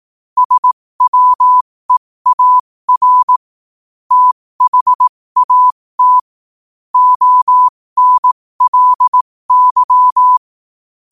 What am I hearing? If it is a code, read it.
Morse code, heard as a keyed tone, about 18 wpm, spaced as for 15 wpm, 1000 Hz. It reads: SWEAR THAT ONLY
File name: qotdfiles / 15wpm